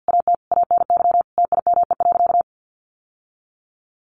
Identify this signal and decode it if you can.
Morse code, heard as a keyed tone, about 50 wpm, 700 Hz. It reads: AT ANY TIME.
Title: Sentences from Top 1100 Words 50wpm